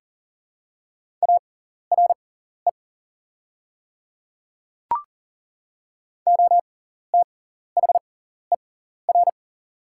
Morse code, heard as a keyed tone, about 40 wpm, spaced as for 14 wpm, 690 Hz. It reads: ARE OTHER
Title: 6x Character Spacing - Top 100 Words - 40wpm